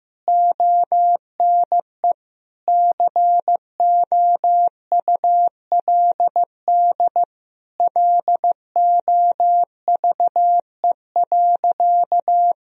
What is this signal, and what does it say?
Morse code, heard as a keyed tone, about 15 wpm, 700 Hz. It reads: ONE COULD LOVE.
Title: Sentences from Top 1000 Words - Review Part 2 - 15wpm